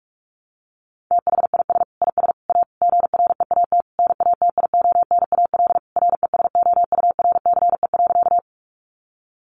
Morse code, heard as a keyed tone, about 44 wpm, 700 Hz. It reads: THIS IS A GREAT NATIONAL RESOURCE.
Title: Sentences from Top 1300 Words 45wpm